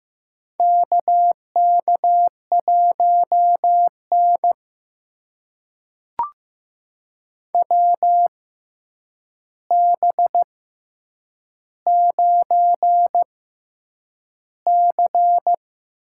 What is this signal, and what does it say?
Morse code, heard as a keyed tone, about 15 wpm, spaced as for 8 wpm, 700 Hz. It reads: KK1N WB9C
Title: USA Call Signs - 6x Character Spacing - 15wpm